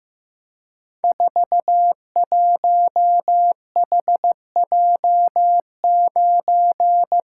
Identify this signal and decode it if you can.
Morse code, heard as a keyed tone, about 15 wpm, 700 Hz. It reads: 41HJ9
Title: Code Groups - 5 Letter:Numbers - 15wpm